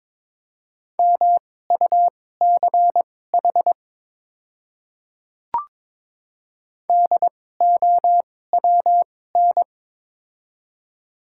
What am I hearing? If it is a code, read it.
Morse code, heard as a keyed tone, about 22 wpm, spaced as for 16 wpm, 700 Hz. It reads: MUCH DOWN
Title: 2x Character Spacing - Top 300 Words - 22wpm